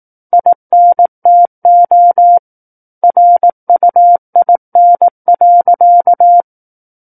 Morse code, heard as a keyed tone, about 18 wpm, 700 Hz. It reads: INTO RUIN.
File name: Book One / 09-Chapter 9 - 18 WPM